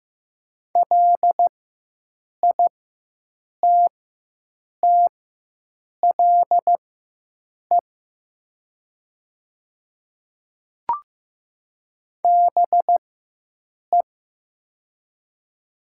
Morse code, heard as a keyed tone, about 15 wpm, spaced as for 7 wpm, 700 Hz. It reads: LITTLE BE
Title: 4x Character Spacing - Top 200 Words - 15wpm